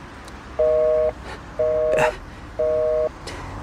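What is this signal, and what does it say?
Busy tone (North America), 480 + 620 Hz.
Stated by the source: Telephone tone from the radio drama Thriller time - Die drei Sonnen 7.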